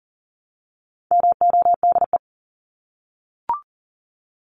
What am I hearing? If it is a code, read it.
Morse code, heard as a keyed tone, about 40 wpm, 700 Hz. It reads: MODE
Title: Top 1751-2000 Words 40wpm